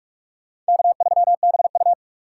Morse code, heard as a keyed tone, about 59 wpm, spaced as for 52 wpm, 700 Hz. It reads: K3BU
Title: Speed Racing - USA Call Signs 45wpm